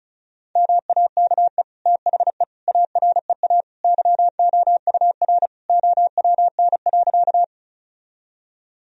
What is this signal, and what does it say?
Morse code, heard as a keyed tone, about 36 wpm, 700 Hz. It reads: MAKE THE AREA YOUR OWN.
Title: Sentences from Top 1000 Words - Review Part 2 - 35wpm